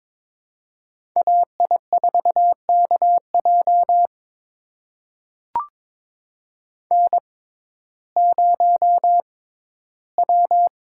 Morse code, heard as a keyed tone, about 23 wpm, spaced as for 14 wpm, 700 Hz. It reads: AI4KJ N0W
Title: USA Call Signs - 5 Characters - 6x Character Spacing - 22wpm